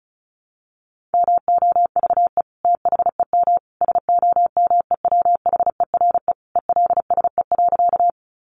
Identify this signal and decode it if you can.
Morse code, heard as a keyed tone, about 34 wpm, 700 Hz. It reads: MOVE THEM SOMEWHERE ELSE.
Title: Sentences from Top 1400 Words 35wpm